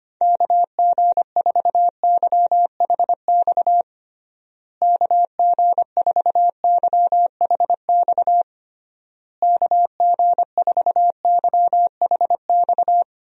Morse code, heard as a keyed tone, about 25 wpm, 700 Hz. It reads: KG4YHX KG4YHX KG4YHX
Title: USA Call Signs - 6 Characters - Repeated 3x - 25wpm